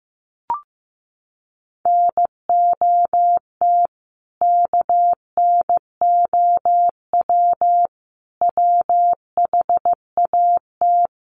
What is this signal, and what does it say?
Morse code, heard as a keyed tone, about 15 wpm, 700 Hz. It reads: NOT KNOW WHAT
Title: Sets of 3 Words 15wpm